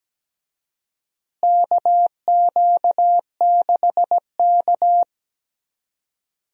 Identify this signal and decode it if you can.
Morse code, heard as a keyed tone, about 17 wpm, 700 Hz. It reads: KQ6K
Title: USA Call Signs - 4 Characters - Repeated 2x - 17wpm